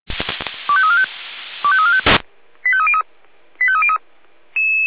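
Five-tone selective call (ZVEI), heard as quick runs of five tones, tones of about 70 ms. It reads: 25345 25345 85303 85303
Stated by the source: Received on SDR, ZVEI code 25343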